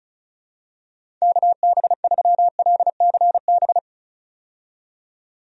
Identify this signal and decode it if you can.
Morse code, heard as a keyed tone, about 34 wpm, 700 Hz. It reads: KB3LCB
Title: USA Call Signs - 6 Characters - 35wpm